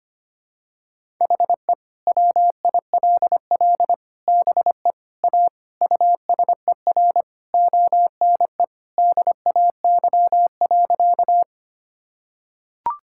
Morse code, heard as a keyed tone, about 25 wpm, 700 Hz. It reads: HE WILL BE A USER ONE DAY.
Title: Sentences from Top 1000 Words - Review Part 5 - 25wpm